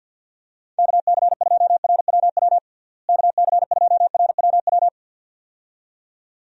Morse code, heard as a keyed tone, about 49 wpm, 700 Hz. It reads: KC2RWW KC2RWW
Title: USA Call Signs - 6 Characters - Repeated 2x - 50wpm